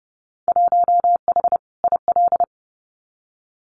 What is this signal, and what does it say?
Morse code, heard as a keyed tone, about 30 wpm, 700 Hz. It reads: 1H IL
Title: Winter Field Day Exchanges 30wpm